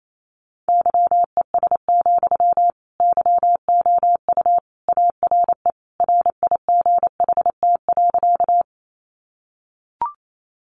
Morse code, heard as a keyed tone, about 28 wpm, 700 Hz. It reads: YES, YOU ARE RIGHT.
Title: Sentences from Top 500 Words 28wpm